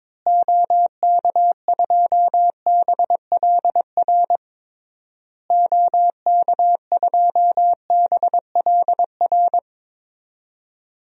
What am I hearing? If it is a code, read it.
Morse code, heard as a keyed tone, about 22 wpm, 700 Hz. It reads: OK2BLR OK2BLR
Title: DXCC-100 Call Signs - 6 Characters - Repeated 3x - 22wpm